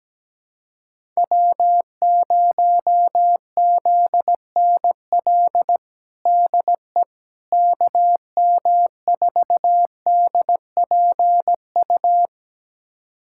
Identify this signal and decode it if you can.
Morse code, heard as a keyed tone, about 17 wpm, 700 Hz. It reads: W0ZNL DE KM4DPU
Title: USA Callsign DE Callsign 17wpm